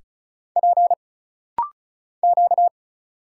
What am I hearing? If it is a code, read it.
Morse code, heard as a keyed tone, about 36 wpm, 700 Hz. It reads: P Q
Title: Single Letters - Rapid-Fire 35wpm